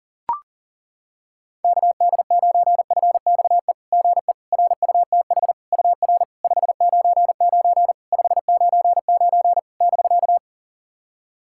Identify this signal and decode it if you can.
Morse code, heard as a keyed tone, about 40 wpm, 700 Hz. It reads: KD9FXE GE RUTH UR 599 599 <BK>
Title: SOTA QSO Practice 40wpm